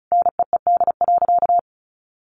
Morse code, heard as a keyed tone, about 34 wpm, 700 Hz. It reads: NEED.
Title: Sentences from Top 1100 Words 35wpm